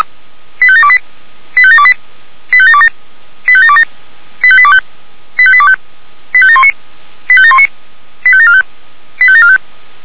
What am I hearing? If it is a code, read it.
Five-tone selective call (ZVEI), heard as quick runs of five tones, tones of about 70 ms. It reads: 85728 85728 85727 85727 85825 85825 85719 85719 85745 85745